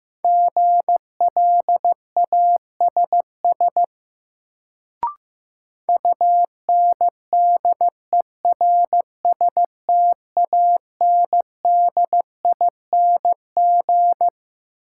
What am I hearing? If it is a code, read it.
Morse code, heard as a keyed tone, about 15 wpm, 700 Hz. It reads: GLASS UNDERSTANDING
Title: Sentences from Top 1000 Words - Review Part 3 - New Word Drill - 15wpm